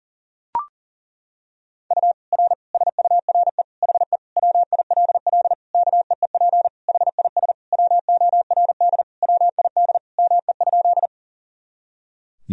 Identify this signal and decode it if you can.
Morse code, heard as a keyed tone, about 40 wpm, 700 Hz. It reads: U R SURE HE WILL KEEP HIS WORD WID ME?
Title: Sentences from Top 300 Words and Abbreviations 40wpm